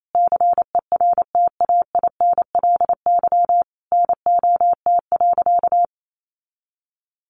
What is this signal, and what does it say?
Morse code, heard as a keyed tone, about 28 wpm, 700 Hz. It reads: CERTAINLY NOT.
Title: Sentences from Top 900 Words 28wpm